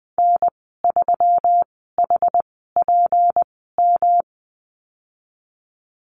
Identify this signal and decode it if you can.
Morse code, heard as a keyed tone, about 20 wpm, spaced as for 15 wpm, 700 Hz. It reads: N3HPM
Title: 161 Lesson 26g - Letter g - Call Signs 20wpm